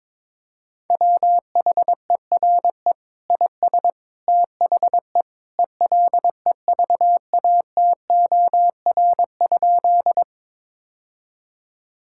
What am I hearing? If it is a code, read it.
Morse code, heard as a keyed tone, about 22 wpm, 700 Hz. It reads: WHERE IS THE ELEVATOR?
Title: Sentences from Top 1600 Words 22wpm